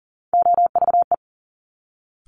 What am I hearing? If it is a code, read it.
Morse code, heard as a keyed tone, about 38 wpm, 700 Hz. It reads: OVE
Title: Top Three Letter Combinations in Words 40wpm